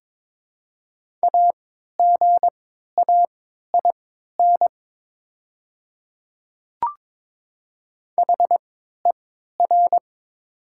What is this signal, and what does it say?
Morse code, heard as a keyed tone, about 22 wpm, spaced as for 13 wpm, 700 Hz. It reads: AGAIN HER
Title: 3x Character Spacing - Top 500 Words - 22wpm